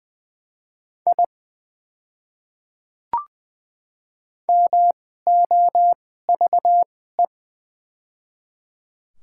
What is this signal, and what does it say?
Morse code, heard as a keyed tone, about 20 wpm, spaced as for 14 wpm, 700 Hz. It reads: I MOVE
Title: 2x Character Spacing - Top 300 Words - 20wpm